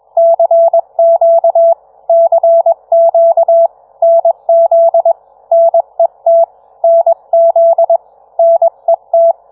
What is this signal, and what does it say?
Morse code, heard as a keyed tone, about 21 wpm, 670 Hz. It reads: CQ CQ NZ NET NZ NET